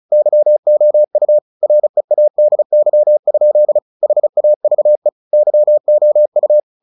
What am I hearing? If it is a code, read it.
Morse code, heard as a keyed tone, about 35 wpm, 600 Hz. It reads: YOU READY? HAVE YOU
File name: the old man and the sea / the old man and the sea 1 35wpm